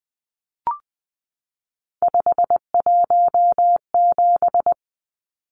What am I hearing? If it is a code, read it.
Morse code, heard as a keyed tone, about 20 wpm, 700 Hz. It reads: 517
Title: Numbers - 3-Digit 20wpm